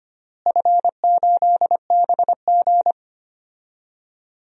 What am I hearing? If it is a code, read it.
Morse code, heard as a keyed tone, about 25 wpm, 700 Hz. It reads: F8BG
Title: DXCC-100 Call Signs - 4 Characters - Repeated 2x - 25wpm